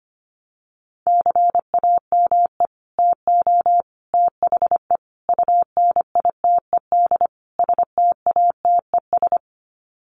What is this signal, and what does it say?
Morse code, heard as a keyed tone, about 25 wpm, 700 Hz. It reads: CAME TO THE UNITED STATES
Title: Sets of 5 Words Encore2 25wpm